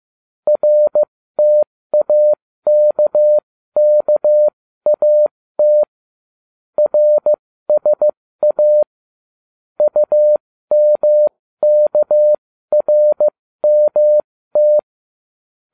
Morse code, heard as a keyed tone, about 15 wpm, spaced as for 13 wpm, 600 Hz. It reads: RTAKKAT RSA UMKRMT